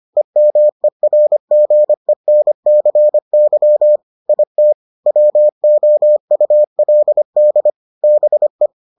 Morse code, heard as a keyed tone, about 25 wpm, 600 Hz. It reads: EMERGENCY IT WOULD BE